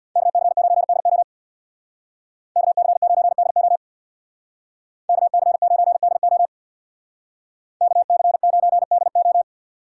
Morse code, heard as a keyed tone, about 64 wpm, 700 Hz. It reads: KC9DO KC9DO KC9DO KC9DO